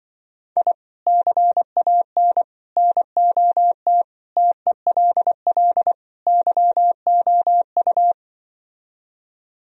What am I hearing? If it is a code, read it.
Morse code, heard as a keyed tone, about 25 wpm, 700 Hz. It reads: I CAN NOT TELL YOU